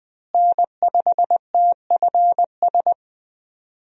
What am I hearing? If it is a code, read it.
Morse code, heard as a keyed tone, about 20 wpm, 700 Hz. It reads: N5TFS
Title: USA Call Signs - 5 Characters - Repeated 2x - 20wpm